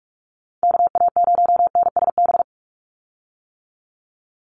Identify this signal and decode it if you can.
Morse code, heard as a keyed tone, about 46 wpm, 700 Hz. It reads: KA0NSB
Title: USA Call Signs 45wpm